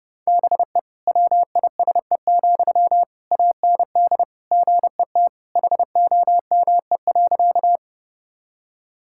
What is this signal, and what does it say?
Morse code, heard as a keyed tone, about 30 wpm, 700 Hz. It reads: BE WISE, AND GET HOME.
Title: Sentences from Top 1600 Words 30wpm